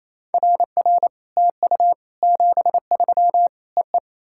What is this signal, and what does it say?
Morse code, heard as a keyed tone, about 28 wpm, 690 Hz. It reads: RR TU 73 EE
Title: SOTA QSO Practice 28wpm